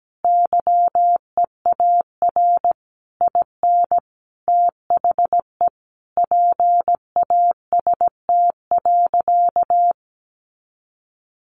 Morse code, heard as a keyed tone, about 17 wpm, 700 Hz. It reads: YEAR IN THE PAST.
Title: A Sound Of Thunder 17wpm